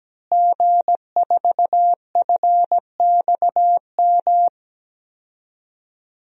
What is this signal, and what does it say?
Morse code, heard as a keyed tone, about 17 wpm, 700 Hz. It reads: G4FXM
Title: DXCC-100 Call Signs - 5 Characters - Repeated 2x - 17wpm